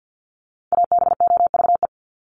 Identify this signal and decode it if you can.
Morse code, heard as a keyed tone, about 50 wpm, 700 Hz. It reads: ABOVE
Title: Top 500 Words 50wpm